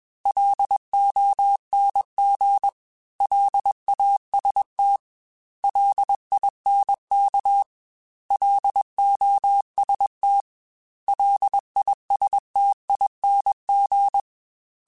Morse code, heard as a keyed tone, about 21 wpm, 800 Hz. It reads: LONG LAST LINK LOST LISTING
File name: L Runs / L Runs mp3s / L Runs @20